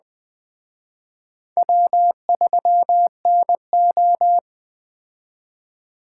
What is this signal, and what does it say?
Morse code, heard as a keyed tone, about 20 wpm, 700 Hz. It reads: W3NO
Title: Speed Racing - USA 1x2 Call Signs 17wpm